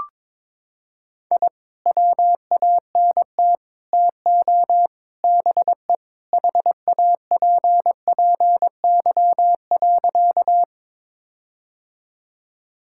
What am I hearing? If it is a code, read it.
Morse code, heard as a keyed tone, about 22 wpm, 700 Hz. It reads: I WANT TO BE HAPPY.
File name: practice / 22wpm / Sentences with 4-Word Phrases - 22wpm